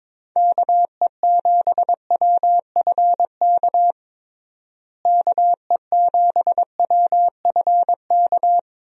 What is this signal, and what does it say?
Morse code, heard as a keyed tone, about 22 wpm, 700 Hz. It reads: KE7WFK KE7WFK
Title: USA Call Signs - 6 Characters - Repeated 2x - 22wpm